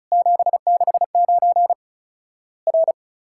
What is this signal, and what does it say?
Morse code, heard as a keyed tone, about 35 wpm, 700 Hz. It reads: 769 R
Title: Worked All Europe DX Contest - QTC Exchange - 35wpm